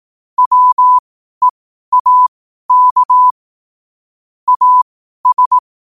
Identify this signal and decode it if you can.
Morse code, heard as a keyed tone, about 18 wpm, spaced as for 12 wpm, 1000 Hz. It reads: WEAK AS
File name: qotdfiles / 12wpm